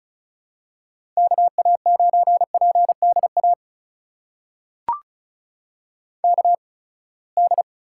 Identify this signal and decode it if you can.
Morse code, heard as a keyed tone, about 36 wpm, spaced as for 24 wpm, 700 Hz. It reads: KA9PDA KD
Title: USA Call Signs - 6 Characters - 8x Character Spacing - 35wpm